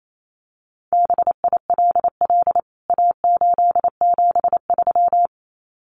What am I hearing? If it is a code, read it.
Morse code, heard as a keyed tone, about 28 wpm, 700 Hz. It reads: BILL A873
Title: CWT Exchange 28wpm